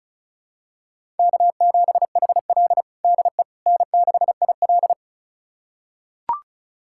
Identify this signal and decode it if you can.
Morse code, heard as a keyed tone, about 36 wpm, 700 Hz. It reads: K7HL DE N6IL
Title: USA Callsign DE Callsign - 4 Characters - 35wpm